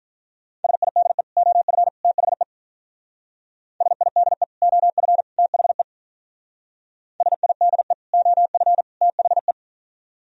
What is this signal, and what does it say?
Morse code, heard as a keyed tone, about 42 wpm, 700 Hz. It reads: SIDE OF THE SIDE OF THE SIDE OF THE
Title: Speed-Racing - Sets of 3 Words 35wpm